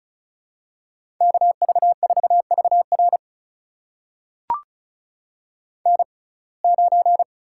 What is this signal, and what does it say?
Morse code, heard as a keyed tone, about 34 wpm, spaced as for 25 wpm, 700 Hz. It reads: KV4VR N9
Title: USA Call Signs - 5 Characters - 6x Character Spacing - 35wpm